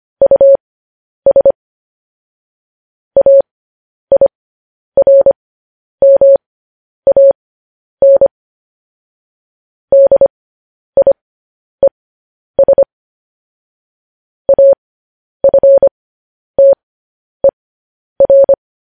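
Morse code, heard as a keyed tone, about 25 wpm, spaced as for 10 wpm, 550 Hz. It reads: US AIRMAN DIES AFTER
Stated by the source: Morse news headlines sent at 10 wpm.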